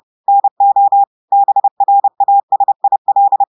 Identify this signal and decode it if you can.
Morse code, heard as a keyed tone, about 30 wpm, 800 Hz. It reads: NO BRASIL
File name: audios / portugais / portugais texte en clair 1 vitesse 30 mots minutes